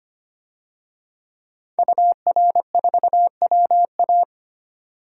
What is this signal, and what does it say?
Morse code, heard as a keyed tone, about 25 wpm, 700 Hz. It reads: UR4WA
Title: DXCC-100 Call Signs - 5 Characters - 25wpm